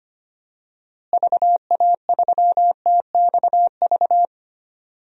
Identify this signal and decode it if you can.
Morse code, heard as a keyed tone, about 25 wpm, 700 Hz. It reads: VA3TXV
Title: DXCC-100 Call Signs - 6 Characters - Repeated 2x - 25wpm